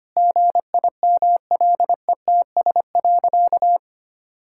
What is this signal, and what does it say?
Morse code, heard as a keyed tone, about 25 wpm, 700 Hz. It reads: GIMLETS.
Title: Pangram Challenge v2 - 25wpm